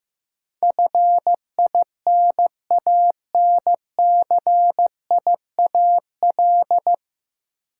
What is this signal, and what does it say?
Morse code, heard as a keyed tone, about 15 wpm, 700 Hz. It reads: FINANCIAL
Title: Sentences from Top 1000 Words - Review Part 5 - New Word Drill - 15wpm